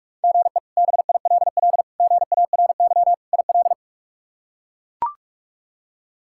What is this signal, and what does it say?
Morse code, heard as a keyed tone, about 42 wpm, 700 Hz. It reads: GE BILL GARY IL
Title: K1USN SST QSO Practice - 45wpm